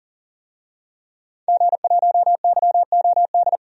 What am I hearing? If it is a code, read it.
Morse code, heard as a keyed tone, about 40 wpm, 700 Hz. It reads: G1YOD
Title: DXCC-100 Call Signs - 5 Characters - 40wpm